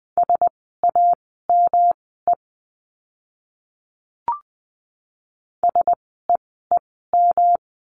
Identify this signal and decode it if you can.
Morse code, heard as a keyed tone, about 20 wpm, spaced as for 15 wpm, 700 Hz. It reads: SAME SEEM
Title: 237 Lesson 38f - Letter q - Words 20wpm